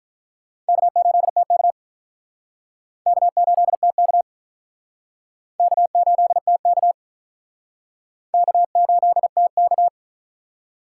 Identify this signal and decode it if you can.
Morse code, heard as a keyed tone, about 48 wpm, 700 Hz. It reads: K8TK K8TK K8TK K8TK